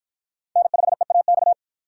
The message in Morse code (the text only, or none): N5EAX